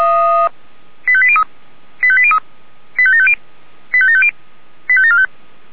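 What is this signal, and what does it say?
Five-tone selective call (ZVEI), heard as quick runs of five tones, tones of about 70 ms. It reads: 85803 85803 85750 85750 85745
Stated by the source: ZVEI five-tone selective call, code 85726, received with an SDR.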